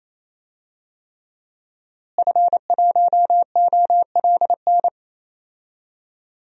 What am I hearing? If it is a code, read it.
Morse code, heard as a keyed tone, about 28 wpm, 700 Hz. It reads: F1OLN
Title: DXCC-100 Call Signs - 5 Characters - Repeated 3x - 28wpm